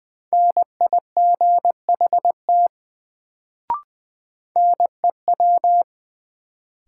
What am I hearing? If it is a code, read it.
Morse code, heard as a keyed tone, about 20 wpm, 700 Hz. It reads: NIGHT NEW
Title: Sentences from Top 300 Words - New Word Drill - 20wpm